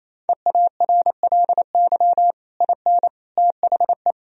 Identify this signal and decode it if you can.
Morse code, heard as a keyed tone, about 28 wpm, 700 Hz. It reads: EARLY IN THE
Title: Sentences from Top 1000 Words - Review Part 2 - 28wpm